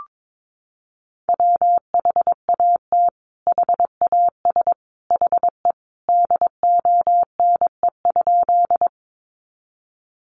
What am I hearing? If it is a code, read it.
Morse code, heard as a keyed tone, about 22 wpm, 700 Hz. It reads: WHAT HAS HE DONE?
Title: Sentences from Top 500 Words 22wpm